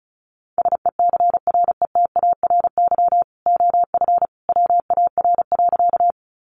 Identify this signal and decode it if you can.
Morse code, heard as a keyed tone, about 34 wpm, 700 Hz. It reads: SECRETARY OF WAR.
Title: Sentences from Top 1300 Words 35wpm